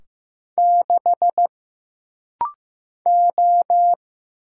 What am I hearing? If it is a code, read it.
Morse code, heard as a keyed tone, about 15 wpm, 700 Hz. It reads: 6 O
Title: Single Letter-Number Alt - Rapid-Fire 15wpm